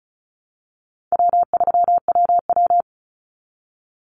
Morse code, heard as a keyed tone, about 34 wpm, 700 Hz. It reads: W3WW
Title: Easy Call Signs Repeated 2x - 35wpm